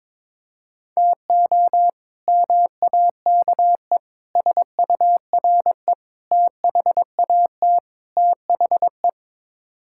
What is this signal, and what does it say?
Morse code, heard as a keyed tone, about 22 wpm, 700 Hz. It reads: TO MAKE SURE THAT THE
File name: practice / 22wpm / Build A Phrase - 5-Word Phrases - 22wpm